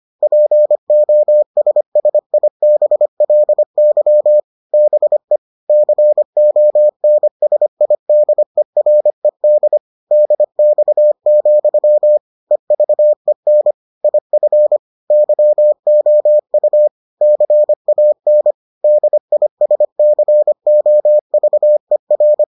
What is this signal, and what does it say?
Morse code, heard as a keyed tone, about 25 wpm, 600 Hz. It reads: POSSIBLY BE CONSIDERED DX, EVEN IF YOU CAN DISCOVER